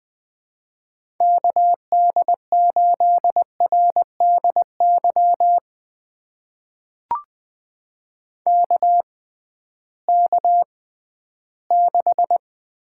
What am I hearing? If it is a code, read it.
Morse code, heard as a keyed tone, about 20 wpm, spaced as for 13 wpm, 700 Hz. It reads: KD8RDY KK6
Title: USA Call Signs - 6 Characters - 6x Character Spacing - 20wpm